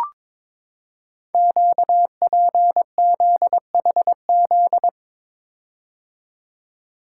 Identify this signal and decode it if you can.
Morse code, heard as a keyed tone, about 22 wpm, 700 Hz. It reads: QPZHZ
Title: Code Groups - 5 Letters - 22wpm